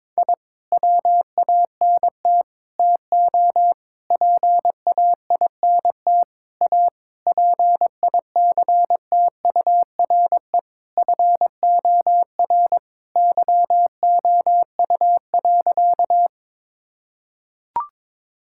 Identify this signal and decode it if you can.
Morse code, heard as a keyed tone, about 22 wpm, 700 Hz. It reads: I WANT TO PAINT A PICTURE FOR YOU.
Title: Sentences from Top 1000 Words - Review Part 6 - 22wpm